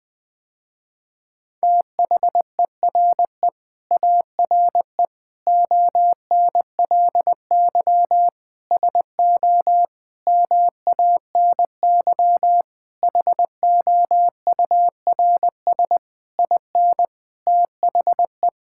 Morse code, heard as a keyed tone, about 20 wpm, 700 Hz. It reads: THERE ARE ONLY SO MANY HOURS IN THE
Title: Sentences with 9-Word Phrases - Spoken Beginning - 20wpm